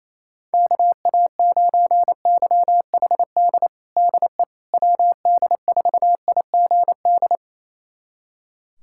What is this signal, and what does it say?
Morse code, heard as a keyed tone, about 28 wpm, 700 Hz. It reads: KA9YHD DE WD4IGD
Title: USA Callsign DE Callsign 28wpm